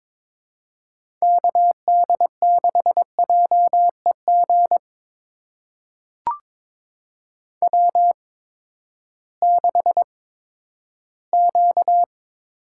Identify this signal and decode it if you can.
Morse code, heard as a keyed tone, about 22 wpm, spaced as for 12 wpm, 700 Hz. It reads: KD6JEG W6Q
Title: USA Call Signs - 8x Character Spacing - 22wpm